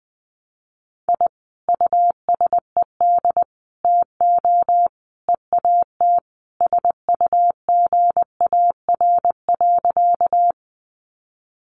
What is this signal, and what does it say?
Morse code, heard as a keyed tone, about 20 wpm, 700 Hz. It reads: I USED TO EAT SUGAR.
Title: Sentences from Top 1200 Words 20wpm